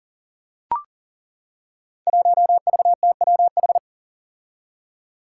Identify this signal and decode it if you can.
Morse code, heard as a keyed tone, about 40 wpm, 700 Hz. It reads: 1VTWH
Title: Five Challenging Letter-Number Combinations 40wpm